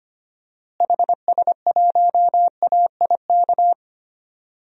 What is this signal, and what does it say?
Morse code, heard as a keyed tone, about 25 wpm, 700 Hz. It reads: HS1AIK
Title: DXCC-100 Call Signs - 6 Characters - Repeated 2x - 25wpm